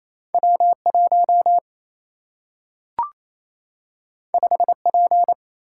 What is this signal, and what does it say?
Morse code, heard as a keyed tone, about 28 wpm, 700 Hz. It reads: W 1 5 P
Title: Single-Letter Single-Number 28wpm